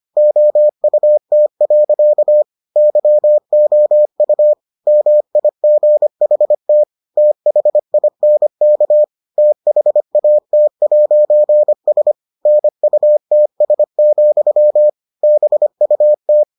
Morse code, heard as a keyed tone, about 25 wpm, 600 Hz. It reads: OUT. YOU MIGHT THINK THAT'S NUTS, BUT